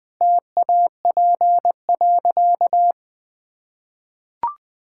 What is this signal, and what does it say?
Morse code, heard as a keyed tone, about 20 wpm, 700 Hz. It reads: TAP.
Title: Easy Sentences with Hints - 20wpm